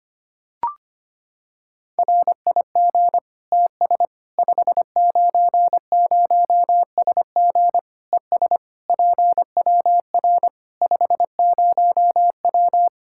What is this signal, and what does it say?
Morse code, heard as a keyed tone, about 25 wpm, 700 Hz. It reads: RIG TS 590SG ES PWR 50W
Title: K1USN SST QSO Practice 25wpm